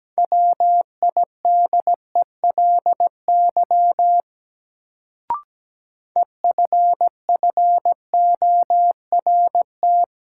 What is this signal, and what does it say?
Morse code, heard as a keyed tone, about 17 wpm, 700 Hz. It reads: WIDELY EFFORT
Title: Sentences from Top 1200 Words - New Word Drill Alt - 17wpm